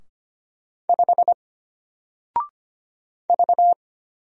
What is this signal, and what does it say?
Morse code, heard as a keyed tone, about 25 wpm, 700 Hz. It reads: 5 V
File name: Single-Letter-Number-25wpm